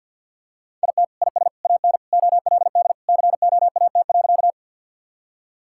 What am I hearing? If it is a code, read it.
Morse code, heard as a keyed tone, about 50 wpm, 700 Hz. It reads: IT IS AN OLD COAT.
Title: Sentences from Top 1600 Words 50wpm